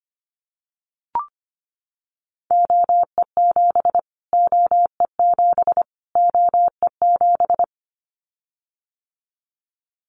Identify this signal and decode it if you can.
Morse code, heard as a keyed tone, about 25 wpm, 700 Hz. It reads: OE7 OE7 OE7
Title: Callsign Prefixes for Top 25 DXCC - Repeated 3x - 25wpm